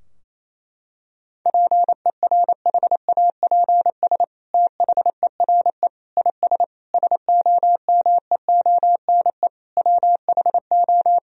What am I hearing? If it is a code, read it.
Morse code, heard as a keyed tone, about 28 wpm, 700 Hz. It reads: PERHAPS THERE IS SOMEONE WHO
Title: Sentences from Top 1000 Words - Review Part 4 - 28wpm